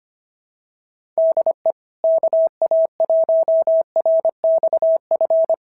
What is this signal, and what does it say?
Morse code, heard as a keyed tone, about 25 wpm, 660 Hz. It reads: DE KA1RXF
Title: SOTA QSO Practice 25wpm